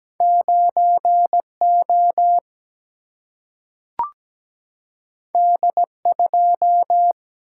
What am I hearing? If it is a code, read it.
Morse code, heard as a keyed tone, about 17 wpm, 700 Hz. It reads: 9 O D 2
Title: Single-Letter Single-Number 17wpm